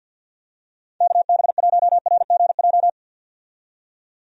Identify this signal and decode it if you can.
Morse code, heard as a keyed tone, about 45 wpm, 700 Hz. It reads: KB1RGJ
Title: USA Call Signs - 6 Characters - 50wpm